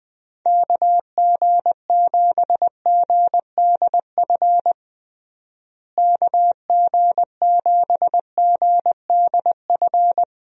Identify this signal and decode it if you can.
Morse code, heard as a keyed tone, about 20 wpm, 700 Hz. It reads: KG7GDF KG7GDF
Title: USA Call Signs - 6 Characters - Repeated 3x - 20wpm